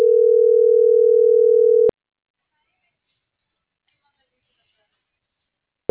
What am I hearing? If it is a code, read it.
Ringback tone (North America), 440 + 480 Hz.
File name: CENTRAL MAYAB / out fernanda.pamela